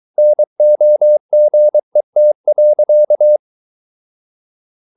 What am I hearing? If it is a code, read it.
Morse code, heard as a keyed tone, about 23 wpm, 600 Hz. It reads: NOGET.
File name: Ferie QRS 22wpm